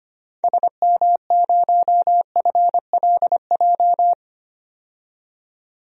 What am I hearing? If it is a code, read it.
Morse code, heard as a keyed tone, about 25 wpm, 700 Hz. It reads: SM0FLJ
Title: DXCC-100 Call Signs - 6 Characters - 25wpm